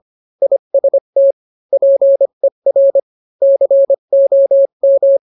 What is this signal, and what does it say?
Morse code, heard as a keyed tone, about 25 wpm, 550 Hz. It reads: IST PER COM